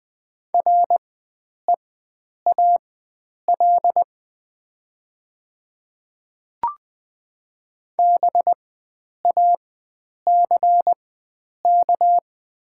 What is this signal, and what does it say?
Morse code, heard as a keyed tone, about 20 wpm, spaced as for 9 wpm, 700 Hz. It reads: REAL BACK
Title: 4x Character Spacing - Top 500 Words - 20wpm